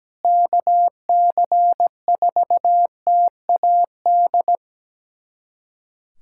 Morse code, heard as a keyed tone, about 17 wpm, 700 Hz. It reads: KC4TAD
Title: USA Callsign DE Callsign - 6 Characters - 17wpm